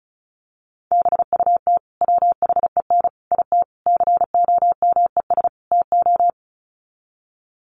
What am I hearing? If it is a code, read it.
Morse code, heard as a keyed tone, about 34 wpm, 700 Hz. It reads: BUT WHEN IT COMES TO
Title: Sets of 5 Words 35wpm